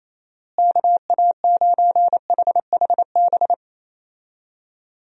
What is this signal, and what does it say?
Morse code, heard as a keyed tone, about 27 wpm, 700 Hz. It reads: KA9HHB
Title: USA Call Signs - 6 Characters - Repeated 3x - 28wpm